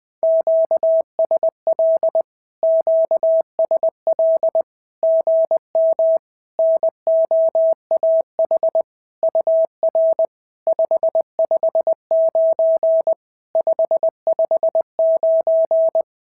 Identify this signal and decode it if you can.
Morse code, heard as a keyed tone, about 20 wpm, 690 Hz. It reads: QSL QSL GM NOAH UR 559 559
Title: SOTA QSO Practice 20wpm